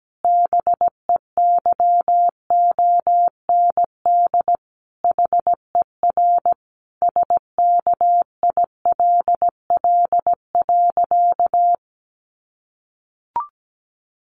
Morse code, heard as a keyed tone, about 17 wpm, 700 Hz. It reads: BEYOND HER SKILL.
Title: Sentences from Top 1200 Words 17wpm